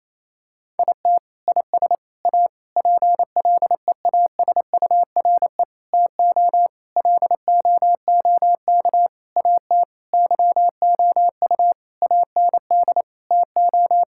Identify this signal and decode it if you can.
Morse code, heard as a keyed tone, about 28 wpm, 700 Hz. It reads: IT IS A PLEASURE TO LOOK AT YOU AND TO
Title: Sentences from Top 1600 Words 28wpm